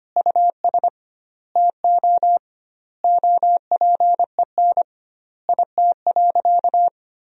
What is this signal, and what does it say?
Morse code, heard as a keyed tone, about 25 wpm, 700 Hz. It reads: US TO OPEN IT.
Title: 2x Word-Spacing - Sentences from Top 300 Words - 25wpm